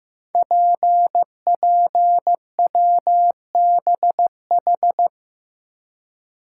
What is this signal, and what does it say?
Morse code, heard as a keyed tone, about 15 wpm, 700 Hz. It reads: PPWBH
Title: Code Groups - 5 Letters - 15wpm